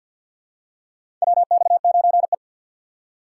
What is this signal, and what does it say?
Morse code, heard as a keyed tone, about 46 wpm, 700 Hz. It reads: WX9E
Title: USA Call Signs - 4 Characters - 2x Character Spacing - 50wpm